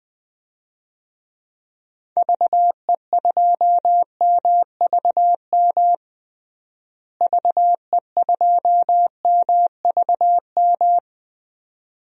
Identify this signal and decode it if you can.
Morse code, heard as a keyed tone, about 20 wpm, 700 Hz. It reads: VE2MVM VE2MVM
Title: DXCC-100 Call Signs - 6 Characters - Repeated 2x - 20wpm